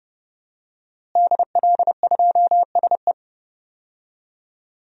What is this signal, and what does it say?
Morse code, heard as a keyed tone, about 32 wpm, 700 Hz. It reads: DL2SE
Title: DXCC-100 Call Signs - 5 Characters - Repeated 2x - 30wpm